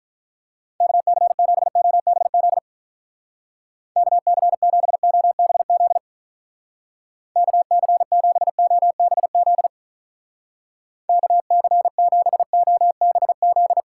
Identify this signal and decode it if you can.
Morse code, heard as a keyed tone, about 43 wpm, 700 Hz. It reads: KC7OBZ KC7OBZ KC7OBZ KC7OBZ
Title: Speed Racing - USA 2x3 Call Signs 35wpm